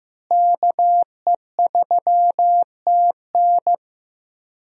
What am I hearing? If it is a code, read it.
Morse code, heard as a keyed tone, about 15 wpm, 700 Hz. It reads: KE3TN
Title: USA Call Signs - 5 Characters - Repeated 3x - 15wpm